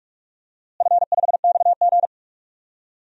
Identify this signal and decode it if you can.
Morse code, heard as a keyed tone, about 45 wpm, 700 Hz. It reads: F5XG
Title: DXCC-100 Call Signs - 4 Characters - Repeated 3x - 45wpm